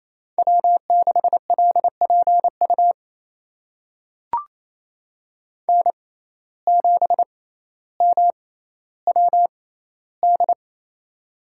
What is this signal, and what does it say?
Morse code, heard as a keyed tone, about 27 wpm, spaced as for 14 wpm, 700 Hz. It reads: W6LPU N7MWD
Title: USA Call Signs - 5 Characters - 6x Character Spacing - 28wpm